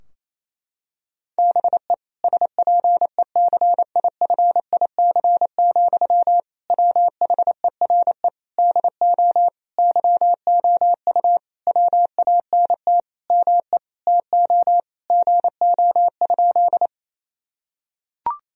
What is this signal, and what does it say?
Morse code, heard as a keyed tone, about 28 wpm, 700 Hz. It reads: BE SPECIFIC, WHERE DO YOU WANT ME TO GO?
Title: Sentences from Top 1000 Words - Review Part 3 - 28wpm